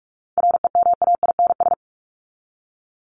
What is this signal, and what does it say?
Morse code, heard as a keyed tone, about 44 wpm, 700 Hz. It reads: REMAINS
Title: ES:S Suffix - Top 100 Words with Base Words 45wpm